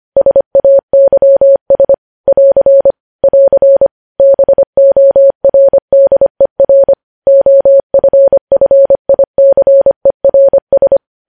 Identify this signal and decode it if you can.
Morse code, heard as a keyed tone, about 25 wpm, 550 Hz. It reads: SAYS <AR> <AR> BORDER OFFICERS
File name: podcast / News headlines-1x-25x25